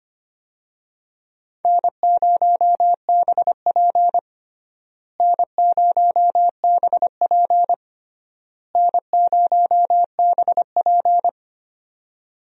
Morse code, heard as a keyed tone, about 25 wpm, 700 Hz. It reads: N0BP N0BP N0BP
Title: USA Call Signs - 4 Characters - Repeated 3x - 25wpm